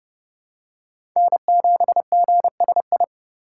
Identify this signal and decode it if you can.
Morse code, heard as a keyed tone, about 29 wpm, 700 Hz. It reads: N7GSI
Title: USA Call Signs - 5 Characters - 2x Character Spacing - 30wpm